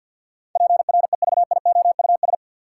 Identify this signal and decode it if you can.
Morse code, heard as a keyed tone, about 50 wpm, 700 Hz. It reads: PREVIOUS
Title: Sentences from Top 1500 Words - New Word Drill Alt - 50wpm